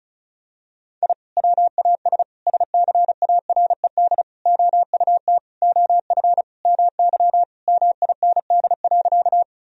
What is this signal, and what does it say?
Morse code, heard as a keyed tone, about 34 wpm, 700 Hz. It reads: I WAS SCARED OUT OF MY MIND.
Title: Sentences from Top 1000 Words - Review Part 4 - 35wpm